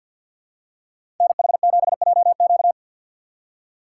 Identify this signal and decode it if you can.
Morse code, heard as a keyed tone, about 52 wpm, 690 Hz. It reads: NH7JQ
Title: USA Call Signs - 5 Characters - 2x Character Spacing - 50wpm